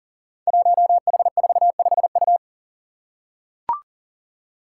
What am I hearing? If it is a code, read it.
Morse code, heard as a keyed tone, about 42 wpm, 700 Hz. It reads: 1H45U
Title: Five Challenging Letter-Number Combinations 40wpm